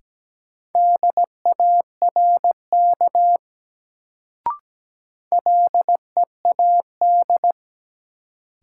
Morse code, heard as a keyed tone, about 17 wpm, 700 Hz. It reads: DARK LEAD
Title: Sentences from Top 500 Words - New Word Drill - 17wpm